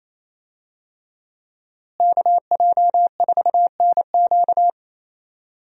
Morse code, heard as a keyed tone, about 28 wpm, 700 Hz. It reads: KJ4NQ
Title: USA Call Signs - 5 Characters - Repeated 3x - 28wpm